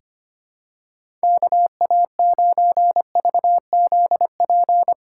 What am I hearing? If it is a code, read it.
Morse code, heard as a keyed tone, about 25 wpm, 700 Hz. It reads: KA9VZP
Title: USA Call Signs - 6 Characters - 8x Character Spacing - 25wpm